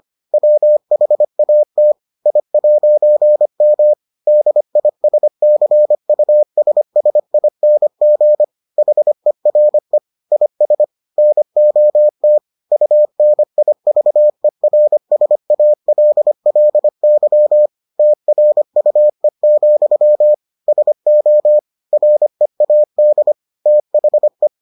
Morse code, heard as a keyed tone, about 25 wpm, 600 Hz. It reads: WHAT I'M DISCUSSING HERE IS NOT UNIVERSALLY TRUE, SO READ THE